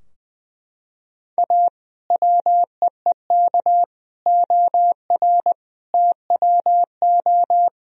Morse code, heard as a keyed tone, about 20 wpm, 700 Hz. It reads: A WEEK OR TWO
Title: Sentences from Top 1000 Words - Review Part 3 - 20wpm